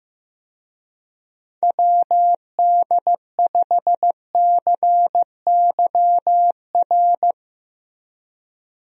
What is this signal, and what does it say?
Morse code, heard as a keyed tone, about 15 wpm, 700 Hz. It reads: WD5CYR
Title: USA Call Signs - 6 Characters - Repeated 3x - 15wpm